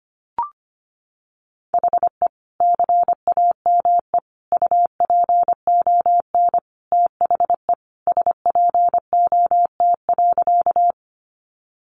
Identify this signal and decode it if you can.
Morse code, heard as a keyed tone, about 25 wpm, 700 Hz. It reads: HE CAME UPON THE SPOT.
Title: Sentences from Top 1100 Words 25wpm